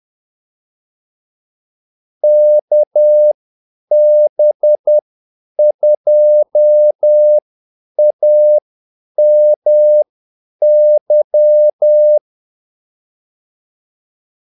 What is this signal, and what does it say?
Morse code, heard as a keyed tone, about 10 wpm, spaced as for 8 wpm, 600 Hz. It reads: KB2AMY
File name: Callsigns-10-8wpm